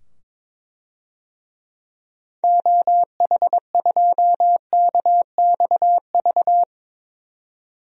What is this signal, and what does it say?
Morse code, heard as a keyed tone, about 22 wpm, 700 Hz. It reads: OH2KXV
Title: DXCC-100 Call Signs - 6 Characters - 22wpm